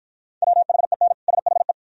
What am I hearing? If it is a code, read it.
Morse code, heard as a keyed tone, about 46 wpm, 700 Hz. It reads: WHEN SHE